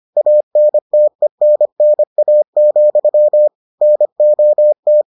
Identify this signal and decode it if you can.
Morse code, heard as a keyed tone, about 25 wpm, 600 Hz. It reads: ANTENNA, NOT